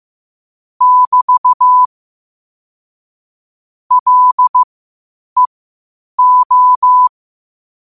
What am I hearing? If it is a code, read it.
Morse code, heard as a keyed tone, about 15 wpm, spaced as for 8 wpm, 1000 Hz. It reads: <BT> LEO